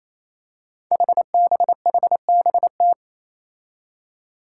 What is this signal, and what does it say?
Morse code, heard as a keyed tone, about 27 wpm, 700 Hz. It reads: HBHBT
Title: Five Challenging Letter-Number Combinations 28wpm